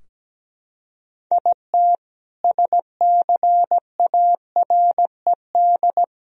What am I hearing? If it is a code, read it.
Morse code, heard as a keyed tone, about 17 wpm, 700 Hz. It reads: IT SCARED
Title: Sentences from Top 1000 Words - Review Part 4 - 17wpm